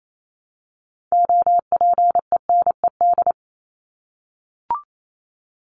Morse code, heard as a keyed tone, about 28 wpm, 700 Hz. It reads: OPENED
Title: ED Suffix - Top 100 Words with Base Words 28wpm